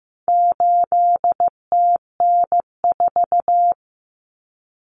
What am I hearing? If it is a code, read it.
Morse code, heard as a keyed tone, about 15 wpm, 700 Hz. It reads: 8TN4
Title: Common Cut Numbers - 4-Digit 15wpm